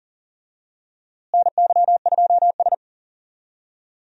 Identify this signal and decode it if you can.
Morse code, heard as a keyed tone, about 40 wpm, 700 Hz. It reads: NY2S
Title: USA Call Signs - 4 Characters - Repeated 2x - 40wpm